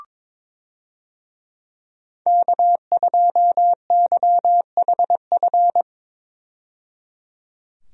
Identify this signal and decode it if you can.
Morse code, heard as a keyed tone, about 22 wpm, 700 Hz. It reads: K2YHF
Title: USA Call Signs - 5 Characters - 22wpm